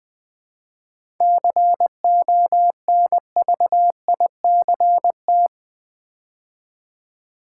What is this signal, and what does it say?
Morse code, heard as a keyed tone, about 20 wpm, 700 Hz. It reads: CONVICT
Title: 7 Letter Words - First 3 Letters Given - 20wpm